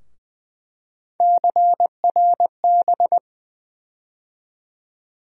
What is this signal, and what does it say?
Morse code, heard as a keyed tone, about 20 wpm, 700 Hz. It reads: CRB
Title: USA Call Sign Postfixes - 3 Letters 20wpm